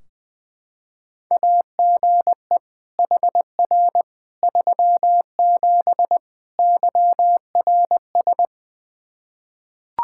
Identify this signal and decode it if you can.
Morse code, heard as a keyed tone, about 21 wpm, 700 Hz. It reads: AGE HR 37 YRS
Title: Practice QSO - Exchange 3 - More Info - 20wpm